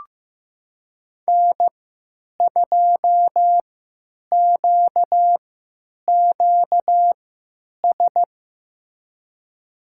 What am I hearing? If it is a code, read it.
Morse code, heard as a keyed tone, about 16 wpm, spaced as for 9 wpm, 700 Hz. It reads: N2QQS
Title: USA Call Signs - 3x Character Spacing - 15wpm